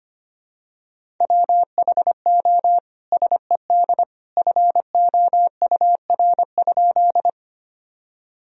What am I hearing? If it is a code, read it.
Morse code, heard as a keyed tone, about 25 wpm, 700 Hz. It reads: WHO SED FOUR?
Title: Sentences from Top 300 Words and Abbreviations 25wpm